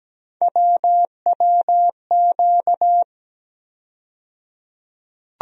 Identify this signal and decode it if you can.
Morse code, heard as a keyed tone, about 17 wpm, 700 Hz. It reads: WWQ
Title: USA Call Sign Postfixes - 3 Letters 17wpm